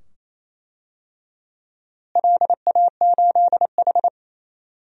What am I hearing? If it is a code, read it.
Morse code, heard as a keyed tone, about 28 wpm, 700 Hz. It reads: LA8H
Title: DXCC-100 Call Signs - 4 Characters - Repeated 2x - 28wpm